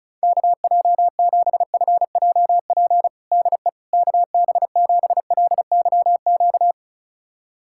Morse code, heard as a keyed tone, about 36 wpm, 700 Hz. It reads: KJ7FJP DE KB7LYQ
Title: USA Callsign DE Callsign - 6 Characters - 35wpm